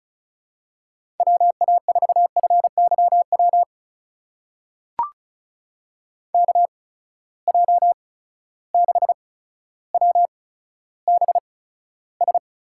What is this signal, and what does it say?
Morse code, heard as a keyed tone, about 36 wpm, spaced as for 15 wpm, 700 Hz. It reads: WA4FYW KJ6WBS
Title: USA Call Signs - 6 Characters - 8x Character Spacing - 35wpm